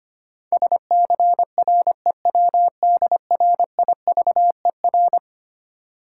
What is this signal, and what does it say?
Morse code, heard as a keyed tone, about 25 wpm, 690 Hz. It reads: SCREWDRIVER
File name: practice / 25wpm / Easy Words with Hints - 25wpm